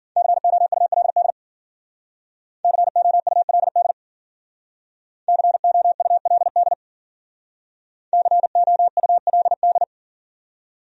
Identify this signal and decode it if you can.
Morse code, heard as a keyed tone, about 52 wpm, 700 Hz. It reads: COULD COULD COULD COULD